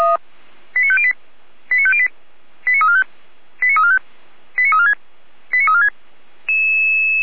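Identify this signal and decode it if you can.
Five-tone selective call (ZVEI), heard as quick runs of five tones, tones of about 70 ms. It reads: 89508 89508 89345 89345 89346 89346
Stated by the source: ZVEI five-tone call received by SDR, from the Feuerwehr Orlamünde (fire brigade) server audio.f85360.feuerwehr-orlamuende.org.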